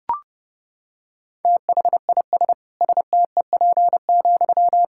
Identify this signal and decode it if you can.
Morse code, heard as a keyed tone, about 32 wpm, 700 Hz. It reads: THIS STEP,
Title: Sentences from Top 1000 Words - Review Part 6 - 30wpm